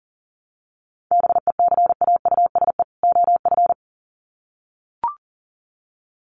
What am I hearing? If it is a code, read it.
Morse code, heard as a keyed tone, about 40 wpm, 700 Hz. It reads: BECAUSE OF
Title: Sets of 2 Words 40wpm